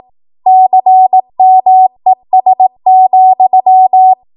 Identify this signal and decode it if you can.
Morse code, heard as a keyed tone, about 18 wpm, 750 Hz. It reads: CMES,
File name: Morse / 18WPM